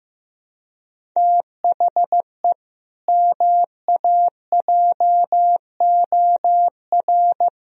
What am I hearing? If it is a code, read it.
Morse code, heard as a keyed tone, about 16 wpm, 700 Hz. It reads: THE MAJOR
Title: Sentences from Top 1000 Words - Review Part 4 - 15wpm